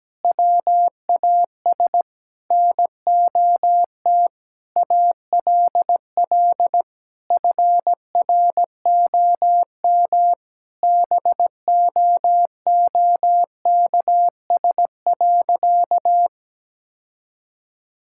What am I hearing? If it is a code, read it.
Morse code, heard as a keyed tone, about 17 wpm, 700 Hz. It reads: WAS NOT ALL FROM BOOKS.
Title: Sentences from Top 1000 Words - Review Part 6 - 17wpm